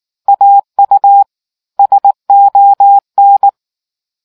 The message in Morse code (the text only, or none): AU SON